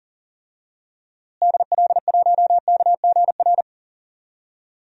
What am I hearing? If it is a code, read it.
Morse code, heard as a keyed tone, about 39 wpm, 700 Hz. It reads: DL1KGR
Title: DXCC-100 Call Signs - 6 Characters - Repeated 3x - 40wpm